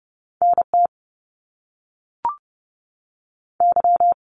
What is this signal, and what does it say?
Morse code, heard as a keyed tone, about 30 wpm, 700 Hz. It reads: NT Y
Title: Canadian Provinces and Territories 30wpm